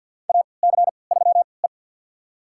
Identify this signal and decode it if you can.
Morse code, heard as a keyed tone, about 53 wpm, spaced as for 29 wpm, 700 Hz. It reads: AC3E
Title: USA Call Signs - 4 Characters - 3x Character Spacing - 50wpm